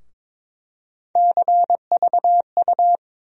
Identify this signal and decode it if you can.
Morse code, heard as a keyed tone, about 22 wpm, 700 Hz. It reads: CVU
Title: USA Call Sign Postfixes - 3 Letters 22wpm